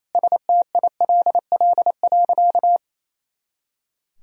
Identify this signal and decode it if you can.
Morse code, heard as a keyed tone, about 28 wpm, 700 Hz. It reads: STILL.
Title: Sentences from Top 500 Words and Abbreviations 28wpm